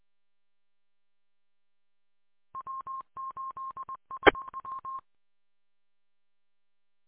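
Morse code, heard as a keyed tone, about 20 wpm, 1060 Hz. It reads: W8SW
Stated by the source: Recorded from the W8UM 145.230 audio archive.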